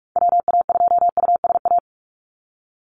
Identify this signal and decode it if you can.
Morse code, heard as a keyed tone, about 44 wpm, 690 Hz. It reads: WA2USA
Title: NA SOTA Chaser Callsigns 45wpm